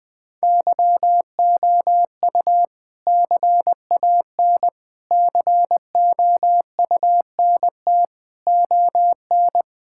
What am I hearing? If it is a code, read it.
Morse code, heard as a keyed tone, about 20 wpm, 700 Hz. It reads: YOU CAN COUNT ON
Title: All-Age Appropriate Jokes - 20wpm